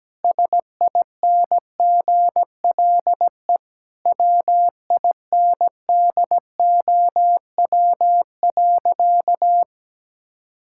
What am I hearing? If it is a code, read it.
Morse code, heard as a keyed tone, about 17 wpm, 700 Hz. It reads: SINGLE WINDOW.
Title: Sentences from Top 1600 Words 17wpm